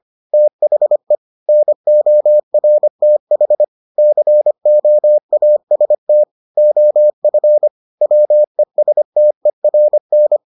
Morse code, heard as a keyed tone, about 25 wpm, 600 Hz. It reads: THE NORTH COAST OF WESTERN